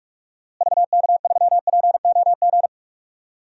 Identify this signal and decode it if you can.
Morse code, heard as a keyed tone, about 41 wpm, 700 Hz. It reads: VK3POG
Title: DXCC-100 Call Signs - 6 Characters - Repeated 2x - 45wpm